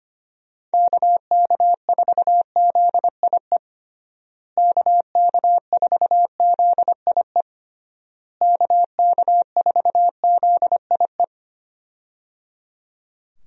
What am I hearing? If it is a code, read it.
Morse code, heard as a keyed tone, about 25 wpm, 700 Hz. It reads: KK4ZIE KK4ZIE KK4ZIE
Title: USA Call Signs - 6 Characters - Repeated 3x - 25wpm